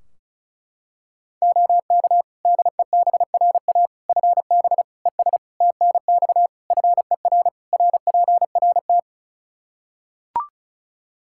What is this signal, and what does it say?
Morse code, heard as a keyed tone, about 36 wpm, 700 Hz. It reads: OK DEBRA FB ES TNX FER RPRT
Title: Practice QSO - Exchange 2 - Info - 35wpm